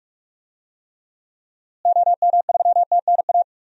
Morse code, heard as a keyed tone, about 46 wpm, 700 Hz. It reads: OM3TNA
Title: DXCC-100 Call Signs - 6 Characters - Repeated 3x - 45wpm